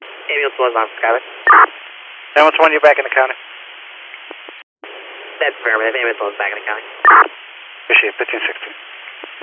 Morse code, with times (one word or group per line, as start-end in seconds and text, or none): none